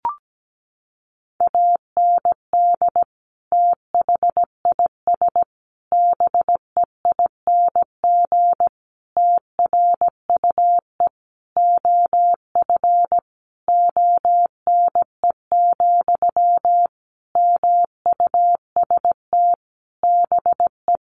Morse code, heard as a keyed tone, about 17 wpm, 700 Hz. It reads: AND THIS BEING TRUE OF ONE, MUST BE
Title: Sentences from Top 1400 Words 17wpm